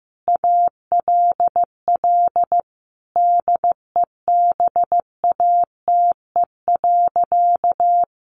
Morse code, heard as a keyed tone, about 14 wpm, 700 Hz. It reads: ALL DEBATE.
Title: Sentences from Top 1100 Words 15wpm